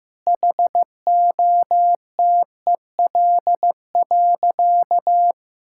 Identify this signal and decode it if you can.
Morse code, heard as a keyed tone, about 15 wpm, 700 Hz. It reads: HOTEL.
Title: Sentences from Top 1600 Words 15wpm